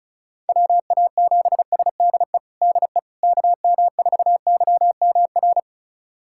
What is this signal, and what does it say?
Morse code, heard as a keyed tone, about 34 wpm, 700 Hz. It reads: WA7SDE DE KM4YMR
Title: USA Callsign DE Callsign - 6 Characters - 35wpm